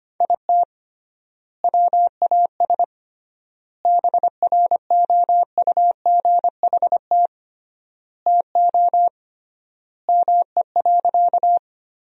Morse code, heard as a keyed tone, about 25 wpm, 700 Hz. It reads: IT WAS BROUGHT TO ME.
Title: 3x Word-Spacing - Sentences from Top 500 Words - 25wpm